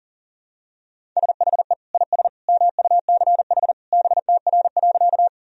Morse code, heard as a keyed tone, about 40 wpm, 700 Hz. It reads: SHE IS MUCH BTR.
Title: Sentences from Top 400 Words and Abbreviations 40wpm